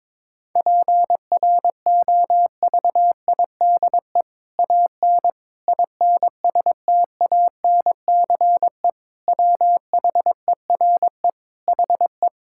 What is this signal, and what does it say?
Morse code, heard as a keyed tone, about 21 wpm, 700 Hz. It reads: PROVIDE AN INSTANCE WHERE HE
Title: Sentences from Top 1000 Words - Review Part 5 - 22wpm